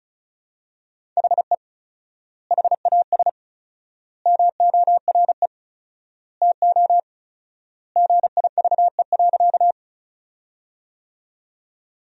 Morse code, heard as a keyed tone, about 34 wpm, 700 Hz. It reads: HE HAS MORE TO GIVE.
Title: 4x Word-Spacing - Sentences from Top 200 Words - 35wpm